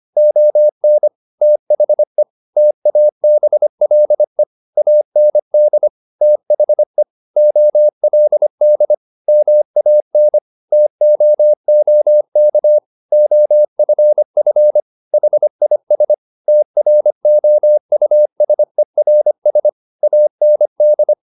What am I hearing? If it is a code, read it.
Morse code, heard as a keyed tone, about 25 wpm, 600 Hz. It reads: ON THE TABLE AND THE OLD MAN TOOK OFF HIS TROUSERS AND
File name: the old man and the sea part 1 25wpm